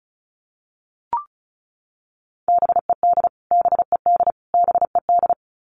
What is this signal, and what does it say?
Morse code, heard as a keyed tone, about 34 wpm, 700 Hz. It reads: BED BED BED
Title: Top 500 Words Repeated 35wpm